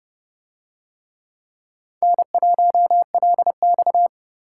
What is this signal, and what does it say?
Morse code, heard as a keyed tone, about 27 wpm, 700 Hz. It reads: N1LX